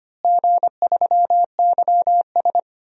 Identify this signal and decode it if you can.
Morse code, heard as a keyed tone, about 25 wpm, 700 Hz. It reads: G3YS
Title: DXCC-100 Call Signs - 4 Characters - Repeated 2x - 25wpm